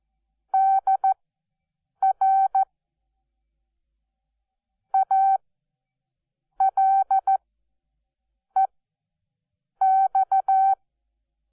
Morse code, heard as a keyed tone, about 14 wpm, spaced as for 6 wpm, 770 Hz. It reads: DR ALEX
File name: Onlinelehrgang / morsen / DJ4UF Morsen12